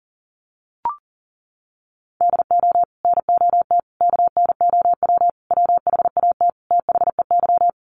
Morse code, heard as a keyed tone, about 40 wpm, 700 Hz. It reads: DO NOT KNOW WHAT THEY
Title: Sets of 5 Words 40wpm